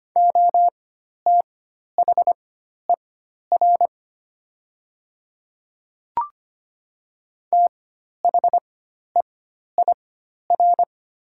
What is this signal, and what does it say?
Morse code, heard as a keyed tone, about 25 wpm, spaced as for 12 wpm, 700 Hz. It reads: OTHER THEIR
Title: 4x Character Spacing - Top 100 Words - 25wpm